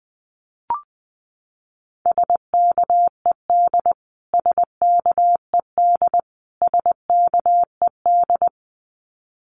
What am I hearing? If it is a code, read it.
Morse code, heard as a keyed tone, about 20 wpm, 700 Hz. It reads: SKED SKED SKED
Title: Abbreviations Repeated 3x - 20wpm